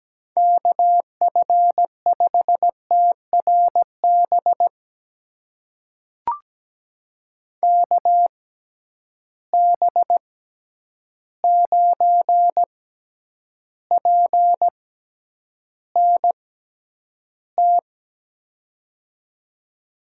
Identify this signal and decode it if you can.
Morse code, heard as a keyed tone, about 17 wpm, spaced as for 9 wpm, 700 Hz. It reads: KF5TRB KB9PNT
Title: USA Call Signs - 6 Characters - 6x Character Spacing - 17wpm